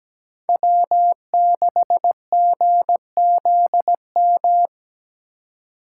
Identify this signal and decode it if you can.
Morse code, heard as a keyed tone, about 20 wpm, 700 Hz. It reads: W6GZM
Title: Speed Racing - USA 1x3 Call Signs 17wpm